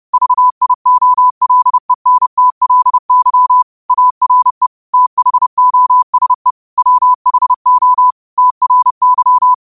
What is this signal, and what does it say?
Morse code, heard as a keyed tone, about 30 wpm, 1000 Hz. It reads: VIOLENTLY ARE THOSE WHO TRY